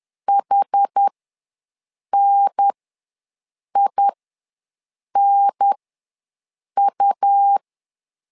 Morse code, heard as a keyed tone, about 10 wpm, spaced as for 6 wpm, 800 Hz. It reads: HNINU
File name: LEZIONI / 2 UHDN / 2 UHDN ws 10wpm 2044